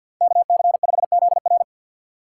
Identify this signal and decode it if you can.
Morse code, heard as a keyed tone, about 52 wpm, 700 Hz. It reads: KC5ZR
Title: USA Call Signs - 5 Characters - 3x Character Spacing - 50wpm